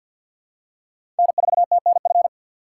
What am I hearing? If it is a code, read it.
Morse code, heard as a keyed tone, about 49 wpm, 700 Hz. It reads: N4TNF
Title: USA Call Signs - 5 Characters - 6x Character Spacing - 50wpm